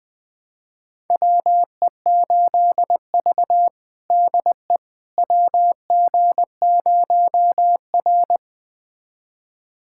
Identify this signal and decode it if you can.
Morse code, heard as a keyed tone, about 20 wpm, 700 Hz. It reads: WE8V DE WG0R
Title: USA Callsign DE Callsign - 4 Characters - 20wpm